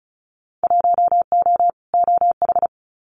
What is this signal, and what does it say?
Morse code, heard as a keyed tone, about 36 wpm, 700 Hz. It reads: 1O OH
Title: Winter Field Day Exchanges 35wpm